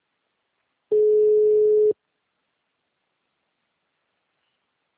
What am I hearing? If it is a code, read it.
Ringback tone (Europe, 425 Hz).
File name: out- -2033- - - .78178